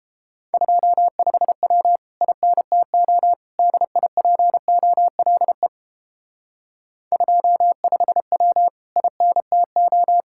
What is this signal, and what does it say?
Morse code, heard as a keyed tone, about 31 wpm, 700 Hz. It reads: 25W INTO DIPOLE 25W INTO